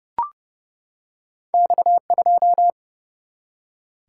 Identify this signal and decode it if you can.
Morse code, heard as a keyed tone, about 30 wpm, 700 Hz. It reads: X2
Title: Single-Letter Single-Number 30wpm